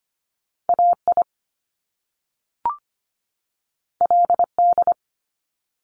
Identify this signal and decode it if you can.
Morse code, heard as a keyed tone, about 25 wpm, 700 Hz. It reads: AI LD
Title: Top Two Letter Combinations in Words 25wpm